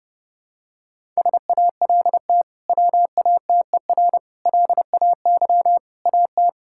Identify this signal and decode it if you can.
Morse code, heard as a keyed tone, about 32 wpm, 700 Hz. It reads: SALT WATER LAY AT
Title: Sentences from Top 1000 Words - Review Part 5 - 30wpm